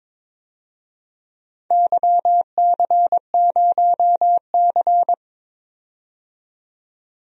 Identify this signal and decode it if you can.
Morse code, heard as a keyed tone, about 22 wpm, 700 Hz. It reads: YC0C
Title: DXCC-100 Call Signs - 4 Characters - 22wpm